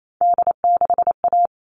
Morse code, heard as a keyed tone, about 28 wpm, 700 Hz. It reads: D6A
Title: All Cut Numbers - 3-Digit 28wpm